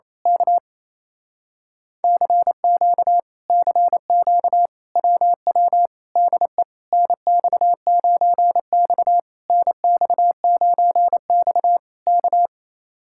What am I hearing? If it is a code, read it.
Morse code, heard as a keyed tone, about 27 wpm, 700 Hz. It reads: K CQ CQ WW DE NX9X NX9X K
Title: Speed-Racing Common QSO Phrases Encore - 28wpm